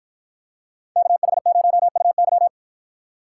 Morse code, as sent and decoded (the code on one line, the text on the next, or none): -.- .... ----- ..- -.--
KH0UY